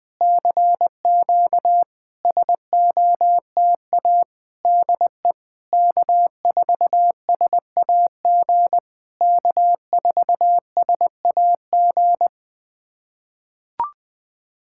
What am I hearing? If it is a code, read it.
Morse code, heard as a keyed tone, about 21 wpm, 690 Hz. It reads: CQ SOTA DE K4SAG K4SAG
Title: SOTA QSO Practice 20wpm